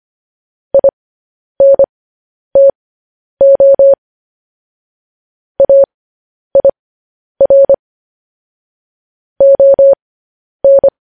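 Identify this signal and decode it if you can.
Morse code, heard as a keyed tone, about 25 wpm, spaced as for 10 wpm, 550 Hz. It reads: INTO AIR ON
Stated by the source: Morse news headlines sent at 10 words per minute.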